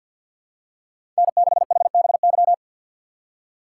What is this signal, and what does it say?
Morse code, heard as a keyed tone, about 47 wpm, 700 Hz. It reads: N6HBY
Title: USA Call Signs - 4x Character Spacing - 50wpm